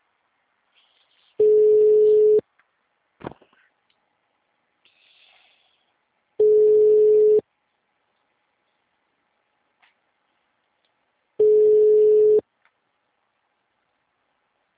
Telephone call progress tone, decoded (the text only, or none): ringback tone